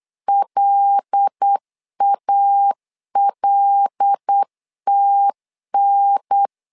Morse code, heard as a keyed tone, about 8 wpm, 800 Hz. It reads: LALTN
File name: LEZIONI / 5 LBVC / 5 L 8wpm 2831